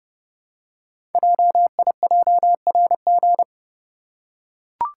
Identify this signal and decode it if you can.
Morse code, heard as a keyed tone, about 30 wpm, 700 Hz. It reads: JIJRG
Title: Code Groups - 5 Letters - 30wpm